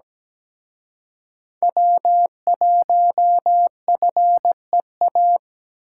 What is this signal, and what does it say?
Morse code, heard as a keyed tone, about 20 wpm, 700 Hz. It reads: W1FEA
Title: Speed Racing - USA 1x3 Call Signs 17wpm